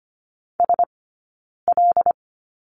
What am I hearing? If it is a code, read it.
Morse code, heard as a keyed tone, about 25 wpm, 700 Hz. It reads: S L
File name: Lesson 17d - Letter u - Two Characters 25wpm